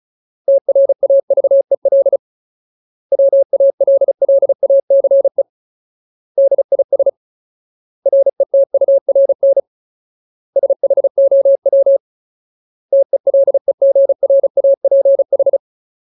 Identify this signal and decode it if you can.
Morse code, heard as a keyed tone, about 34 wpm, 550 Hz. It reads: TRAVEL WALLACE DIS RETURN SHOW TELEGRAPH
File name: Ultimate Word List / 35WPM Ultimate